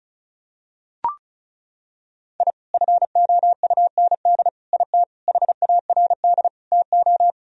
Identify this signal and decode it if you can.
Morse code, heard as a keyed tone, about 36 wpm, 700 Hz. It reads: I FOUND IT HARD TO
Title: Sentences from Top 1000 Words - Review Part 1 - 35wpm